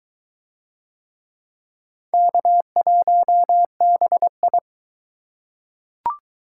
Morse code, heard as a keyed tone, about 21 wpm, 700 Hz. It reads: K1BI